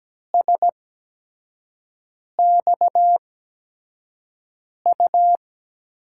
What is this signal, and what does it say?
Morse code, heard as a keyed tone, about 17 wpm, spaced as for 5 wpm, 700 Hz. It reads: SXU
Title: USA Call Signs - 6 Characters - 8x Character Spacing - 17wpm